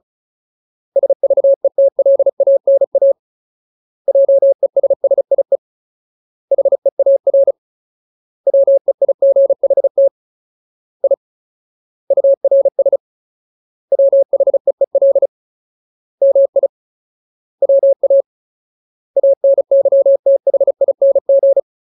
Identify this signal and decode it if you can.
Morse code, heard as a keyed tone, about 34 wpm, 550 Hz. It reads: SVETLANA JESSIE HEAR WEIGHT I URS WHEEL MI WA ANYTHING
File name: Ultimate Word List / 35WPM Ultimate